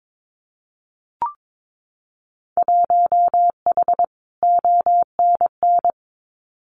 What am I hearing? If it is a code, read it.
Morse code, heard as a keyed tone, about 22 wpm, 700 Hz. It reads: 1H ONN
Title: Winter Field Day Exchanges 22wpm